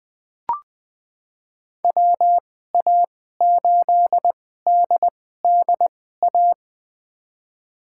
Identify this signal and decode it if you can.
Morse code, heard as a keyed tone, about 20 wpm, spaced as for 14 wpm, 700 Hz. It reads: WA8DDA
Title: USA Call Signs - 2x Character Spacing - 20wpm